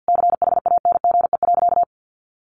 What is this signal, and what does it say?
Morse code, heard as a keyed tone, about 50 wpm, 700 Hz. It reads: CHANGE.
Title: Sentences from Top 1500 Words 50wpm